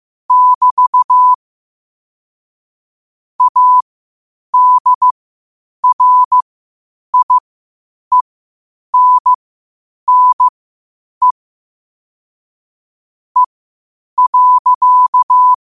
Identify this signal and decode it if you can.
Morse code, heard as a keyed tone, about 15 wpm, spaced as for 8 wpm, 1000 Hz. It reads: <BT> ADRIENNE E.
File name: qotdfiles / 8wpm